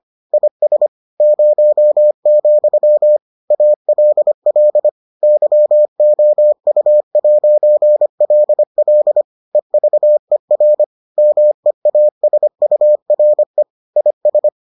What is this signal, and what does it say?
Morse code, heard as a keyed tone, about 25 wpm, 600 Hz. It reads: IS 0, ALL YOU'LL EVER MEASURE IS